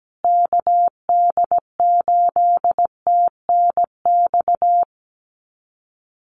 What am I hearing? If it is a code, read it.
Morse code, heard as a keyed tone, about 17 wpm, 700 Hz. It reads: KD8TNX
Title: USA Call Signs Repeated 3x 17wpm